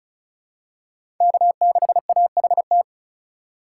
Morse code, heard as a keyed tone, about 36 wpm, 700 Hz. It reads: K6AHT
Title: USA Call Signs - 5 Characters - 3x Character Spacing - 35wpm